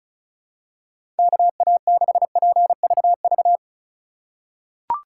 Morse code, heard as a keyed tone, about 36 wpm, 700 Hz. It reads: KA6PVV
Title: USA Call Signs - 6 Characters - 3x Character Spacing - 35wpm